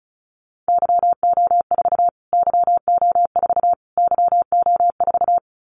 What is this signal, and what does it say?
Morse code, heard as a keyed tone, about 34 wpm, 700 Hz. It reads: YO4 YO4 YO4
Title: Callsign Prefixes for Top 25 DXCC - Repeated 3x - 35wpm